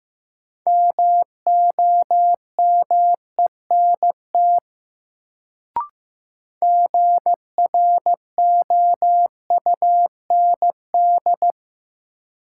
Morse code, heard as a keyed tone, about 15 wpm, 700 Hz. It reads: MOMENT GROUND
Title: Sets of 3 Words - New Word Drill - 15wpm